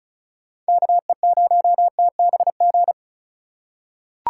Morse code, heard as a keyed tone, about 34 wpm, 700 Hz. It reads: KE0TBG
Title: USA Call Signs - 6 Characters - 3x Character Spacing - 35wpm